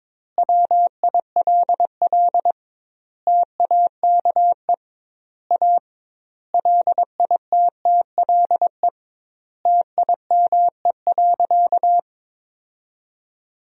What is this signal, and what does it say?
Morse code, heard as a keyed tone, about 22 wpm, 700 Hz. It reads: WILL TAKE A LITTLE TIME.
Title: 2x Word-Spacing - Sentences from Top 200 Words - 22wpm